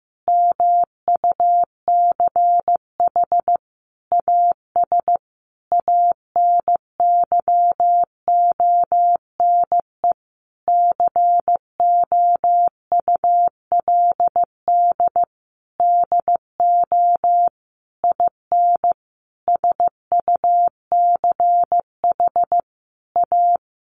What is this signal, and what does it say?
Morse code, heard as a keyed tone, about 15 wpm, 700 Hz. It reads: MUCH AS ANYONE COULD DO IN SUCH A
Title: Sentences from Top 1200 Words 15wpm